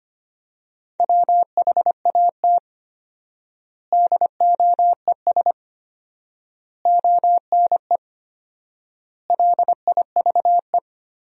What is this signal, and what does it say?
Morse code, heard as a keyed tone, about 25 wpm, 700 Hz. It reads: WHAT DOES ONE LIVE
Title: 4x Word-Spacing - Sentences from Top 200 Words - 25wpm